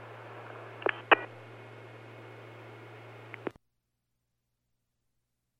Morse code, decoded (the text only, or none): none